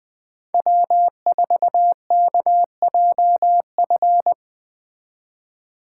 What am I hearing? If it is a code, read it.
Morse code, heard as a keyed tone, about 20 wpm, 700 Hz. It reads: W4KJF